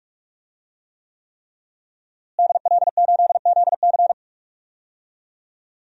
Morse code, heard as a keyed tone, about 46 wpm, 700 Hz. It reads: DL8ZC
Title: DXCC-100 Call Signs - 5 Characters - Repeated 3x - 45wpm